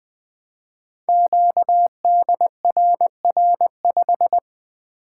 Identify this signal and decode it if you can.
Morse code, heard as a keyed tone, about 20 wpm, 700 Hz. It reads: QDRR5
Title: Code Groups - 5 Letter:Numbers - 20wpm